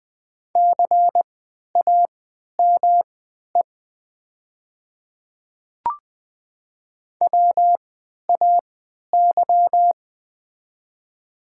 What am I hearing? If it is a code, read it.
Morse code, heard as a keyed tone, about 20 wpm, spaced as for 11 wpm, 690 Hz. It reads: CAME WAY
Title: 3x Character Spacing - Top 200 Words - 20wpm